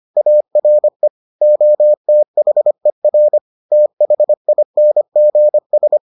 Morse code, heard as a keyed tone, about 25 wpm, 600 Hz. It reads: ARE OTHER THINGS